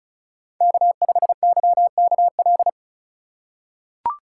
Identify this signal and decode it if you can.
Morse code, heard as a keyed tone, about 34 wpm, 700 Hz. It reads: K5YKL
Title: USA Call Signs - 5 Characters - 3x Character Spacing - 35wpm